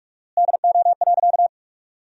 Morse code, heard as a keyed tone, about 46 wpm, 700 Hz. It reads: DO.
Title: 4x Word-Spacing - Sentences from Top 100 Words - 45wpm